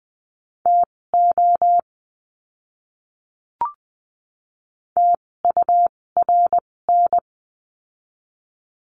Morse code, heard as a keyed tone, about 20 wpm, spaced as for 16 wpm, 700 Hz. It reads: TO TURN
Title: 256 Lesson 42a - Words 12-20wpm